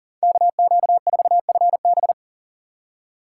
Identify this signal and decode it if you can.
Morse code, heard as a keyed tone, about 39 wpm, 700 Hz. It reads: KQ4FB
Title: USA Call Signs - 5 Characters - 2x Character Spacing - 40wpm